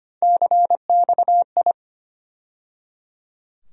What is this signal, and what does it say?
Morse code, heard as a keyed tone, about 25 wpm, 700 Hz. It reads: CXI
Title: USA Call Sign Postfixes - 3 Letters 25wpm